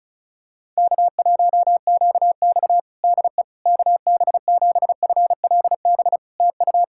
Morse code, heard as a keyed tone, about 34 wpm, 700 Hz. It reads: K1QX DE KB7FLB TU
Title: Practice QSO - Exchange 4 - Ending - 35wpm